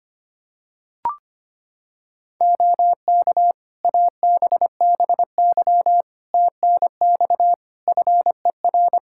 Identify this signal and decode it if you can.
Morse code, heard as a keyed tone, about 25 wpm, 700 Hz. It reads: OK ABBY TNX FER
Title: Practice QSO - Exchange 4 - Ending - 25wpm